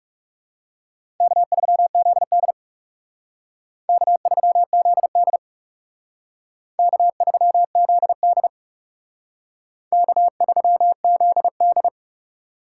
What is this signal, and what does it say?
Morse code, heard as a keyed tone, about 36 wpm, 700 Hz. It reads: K3ZD K3ZD K3ZD K3ZD
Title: Speed Racing - USA 1x2 Call Signs 30wpm